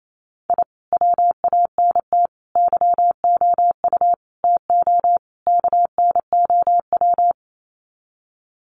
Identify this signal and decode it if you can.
Morse code, heard as a keyed tone, about 28 wpm, 700 Hz. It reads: I WANT YOU TO KNOW
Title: Sets of 5 Words Encore 28wpm